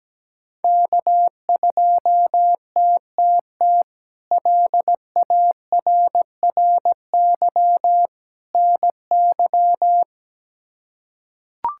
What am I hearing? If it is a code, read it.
Morse code, heard as a keyed tone, about 17 wpm, 700 Hz. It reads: K2TTT LARRY NY
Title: K1USN SST QSO Practice - 17wpm